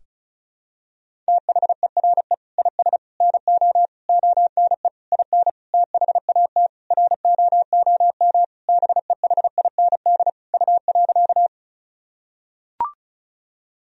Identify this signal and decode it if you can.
Morse code, heard as a keyed tone, about 34 wpm, 700 Hz. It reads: THERE IS NO ONE IN THAT ROOM BEHIND U.
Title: Sentences from Top 500 Words and Abbreviations 35wpm